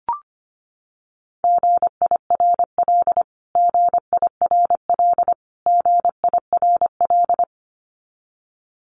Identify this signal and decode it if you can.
Morse code, heard as a keyed tone, about 25 wpm, 700 Hz. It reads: GIRL GIRL GIRL
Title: Top 500 Words Repeated 25wpm